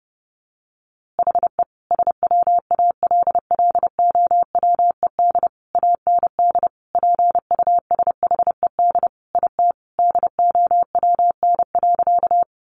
Morse code, heard as a keyed tone, about 30 wpm, 700 Hz. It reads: HE SWALLOWED AND PUSHED IT DOWN.
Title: A Sound Of Thunder 30wpm